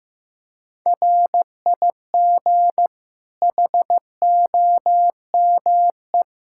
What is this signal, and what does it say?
Morse code, heard as a keyed tone, about 16 wpm, 700 Hz. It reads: RIG HOME
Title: Practice QSO - Exchange 2 - Info - 15wpm